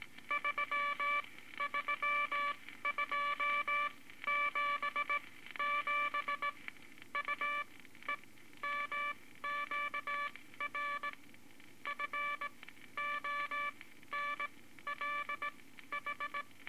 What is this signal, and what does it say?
Morse code, heard as a keyed tone, about 19 wpm, spaced as for 13 wpm, 1160 Hz, with noise. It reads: 33277 UEMQR FONLH
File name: Morse Code Records / 18 B LCM 165 Ten Words Per Minute